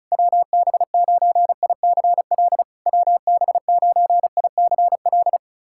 Morse code, heard as a keyed tone, about 36 wpm, 690 Hz. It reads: WB9ICL WB9ICL
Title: SOTA QSO Practice 35wpm